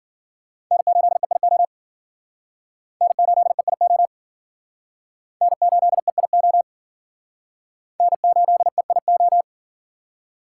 Morse code, heard as a keyed tone, about 50 wpm, 700 Hz. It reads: N8EIO N8EIO N8EIO N8EIO